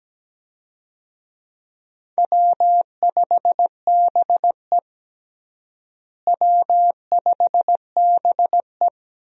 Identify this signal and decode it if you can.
Morse code, heard as a keyed tone, about 17 wpm, 700 Hz. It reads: W5BE W5BE
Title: USA Call Signs - 4 Characters - Repeated 2x - 17wpm